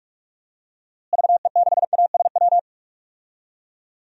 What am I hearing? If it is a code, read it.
Morse code, heard as a keyed tone, about 41 wpm, 700 Hz. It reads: VE6ASW
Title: DXCC-100 Call Signs - 6 Characters - Repeated 2x - 45wpm